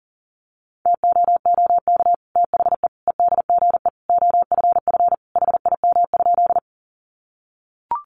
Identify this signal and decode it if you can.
Morse code, heard as a keyed tone, about 40 wpm, 700 Hz. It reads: TOOK THE EDGE OFF HIM?
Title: Sentences from Top 1300 Words 40wpm